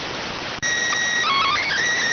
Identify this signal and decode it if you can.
SSTV (slow-scan television), PD 290.